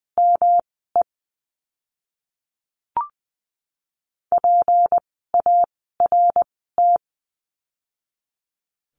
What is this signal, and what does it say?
Morse code, heard as a keyed tone, about 19 wpm, spaced as for 14 wpm, 700 Hz. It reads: ME PART
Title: 206 Lesson 33f - Letter j - Words 20wpm